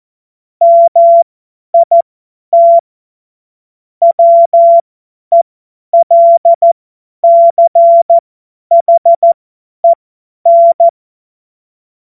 Morse code, heard as a keyed tone, about 14 wpm, spaced as for 10 wpm, 680 Hz. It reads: MIT WELCHEN
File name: Morsekurs / Witze / Witz 21. Juni 2022 50 F70